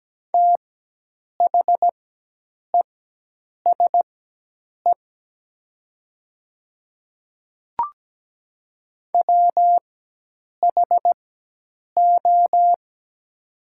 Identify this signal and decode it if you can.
Morse code, heard as a keyed tone, about 17 wpm, spaced as for 8 wpm, 700 Hz. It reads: THESE WHO
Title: 4x Character Spacing - Top 100 Words - 17wpm